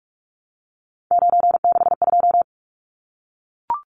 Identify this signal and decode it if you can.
Morse code, heard as a keyed tone, about 44 wpm, 700 Hz. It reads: 962
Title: Numbers - 3-Digit 45wpm